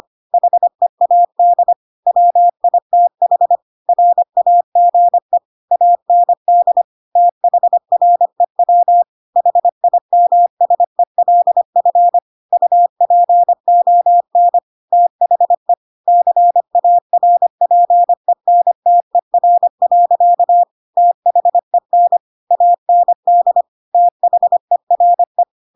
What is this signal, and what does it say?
Morse code, heard as a keyed tone, about 25 wpm, 700 Hz. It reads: HEAD WITH RAGE AND THREW HIMSELF UPON THE CARPENTER. THEN AND THERE
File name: Pinocchio / Pinocchio ch2 25 25 ik1hge